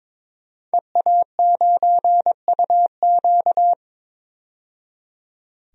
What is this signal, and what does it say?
Morse code, heard as a keyed tone, about 21 wpm, 700 Hz. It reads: EA9UQ
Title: DXCC-100 Call Signs - 5 Characters - 22wpm